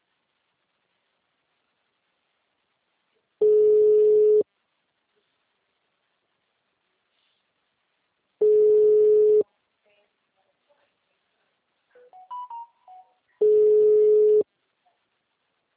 Ringback tone (Europe, 425 Hz).
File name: out- -2056- - - .21106